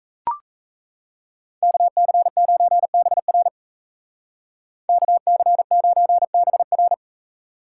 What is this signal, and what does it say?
Morse code, heard as a keyed tone, about 35 wpm, 700 Hz. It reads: KC9BR KC9BR
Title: Speed Racing - USA 2x2 Call Signs 28wpm